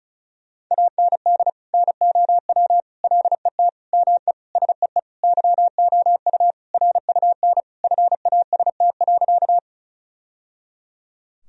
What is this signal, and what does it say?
Morse code, heard as a keyed tone, about 36 wpm, 700 Hz. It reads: AND NOW LET ME SEE YOU RUN FAST.
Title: Sentences from Top 1000 Words - Review Part 2 - 35wpm